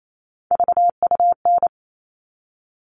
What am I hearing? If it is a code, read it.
Morse code, heard as a keyed tone, about 28 wpm, 700 Hz. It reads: VUN
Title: All Cut Numbers - 3-Digit 28wpm